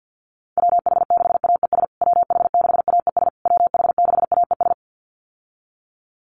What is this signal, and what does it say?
Morse code, heard as a keyed tone, about 50 wpm, 700 Hz. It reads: WH6AES WH6AES WH6AES
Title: USA Call Signs Repeated 3x 50wpm